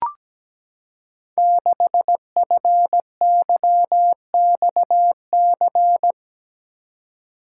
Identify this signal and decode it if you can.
Morse code, heard as a keyed tone, about 17 wpm, 700 Hz. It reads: 6FYXC
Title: Code Groups - 5 Letter:Numbers - 17wpm